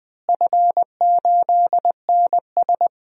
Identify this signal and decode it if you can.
Morse code, heard as a keyed tone, about 21 wpm, 700 Hz. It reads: F8NS
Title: DXCC-100 Call Signs - 4 Characters - Repeated 2x - 20wpm